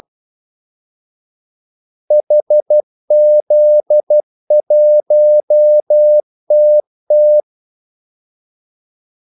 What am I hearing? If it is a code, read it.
Morse code, heard as a keyed tone, about 12 wpm, 600 Hz. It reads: HZ1TT
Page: 2 ekstra ordmellemrum mellem kaldesignalerne